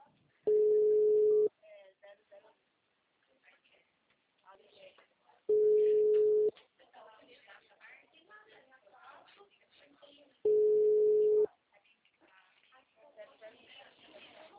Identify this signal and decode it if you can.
Ringback tone (Europe, 425 Hz), with speech.